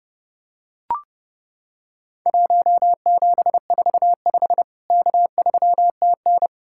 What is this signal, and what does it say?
Morse code, heard as a keyed tone, about 30 wpm, 700 Hz. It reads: 1745 K3TN
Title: Worked All Europe DX Contest - QTC Exchange - 30wpm